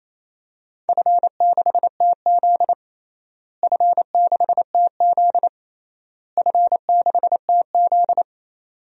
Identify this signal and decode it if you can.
Morse code, heard as a keyed tone, about 27 wpm, 700 Hz. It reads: F6TZ F6TZ F6TZ
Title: DXCC-100 Call Signs - 4 Characters - Repeated 3x - 28wpm